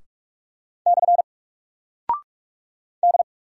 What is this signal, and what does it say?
Morse code, heard as a keyed tone, about 44 wpm, 700 Hz. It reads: / D
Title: Single Character (Letter:Number:Punctuation:Prosign) 45wpm